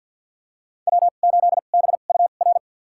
Morse code, heard as a keyed tone, about 48 wpm, spaced as for 35 wpm, 700 Hz. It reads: W8BUR
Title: USA Call Signs - 5 Characters - 2x Character Spacing - 50wpm